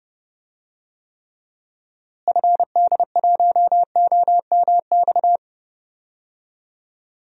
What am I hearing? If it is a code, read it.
Morse code, heard as a keyed tone, about 31 wpm, 700 Hz. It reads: FD1OMX
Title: DXCC-100 Call Signs - 6 Characters - Repeated 2x - 30wpm